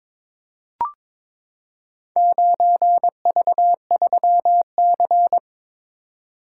Morse code, heard as a keyed tone, about 23 wpm, 700 Hz. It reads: 9V3C
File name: practice / 22wpm / Four Letter-Number Combinations - 22wpm